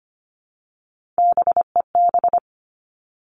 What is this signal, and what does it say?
Morse code, heard as a keyed tone, about 25 wpm, 700 Hz. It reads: BEB
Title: All Cut Numbers - 3-Digit 25wpm